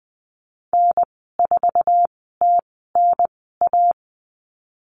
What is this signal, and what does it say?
Morse code, heard as a keyed tone, about 20 wpm, spaced as for 14 wpm, 700 Hz. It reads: N4TNA
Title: 048 Lesson 9g - Number 4 - Call Signs 20wpm